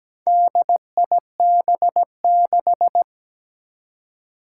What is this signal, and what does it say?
Morse code, heard as a keyed tone, about 17 wpm, 700 Hz. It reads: DIB6
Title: Four Challenging Letter-Number Combinations 17wpm